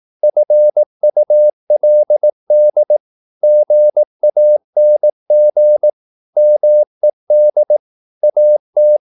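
Morse code, heard as a keyed tone, about 18 wpm, 600 Hz. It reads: FULD GANG MED AT
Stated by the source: Slow (QRS) Morse practice at 18 wpm, CW med Gnister.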